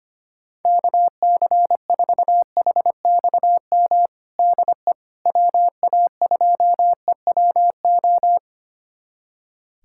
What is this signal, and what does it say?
Morse code, heard as a keyed tone, about 25 wpm, 700 Hz. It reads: KC4HXM DE WA2EWO
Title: USA Callsign DE Callsign 25wpm